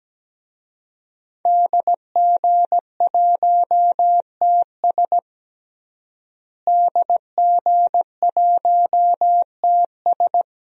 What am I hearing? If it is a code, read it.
Morse code, heard as a keyed tone, about 17 wpm, 700 Hz. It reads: DG1TS DG1TS
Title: DXCC-100 Call Signs - 5 Characters - Repeated 2x - 17wpm